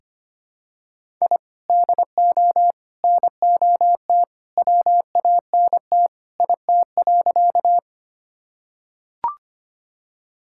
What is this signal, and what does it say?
Morse code, heard as a keyed tone, about 25 wpm, 700 Hz. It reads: I DO NOT WANT IT.
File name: practice / 25wpm / Sentences with 2-Word Phrases - 25wpm